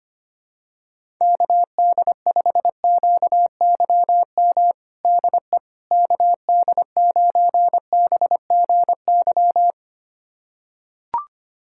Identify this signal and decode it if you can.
Morse code, heard as a keyed tone, about 25 wpm, 700 Hz. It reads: KD5QYM DE KD9BGY
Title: USA Callsign DE Callsign - 6 Characters - 25wpm